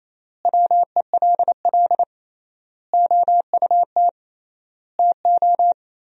Morse code, heard as a keyed tone, about 28 wpm, 700 Hz. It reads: WELL OUT TO
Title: 3x Word-Spacing - Sentences from Top 400 Words - 28wpm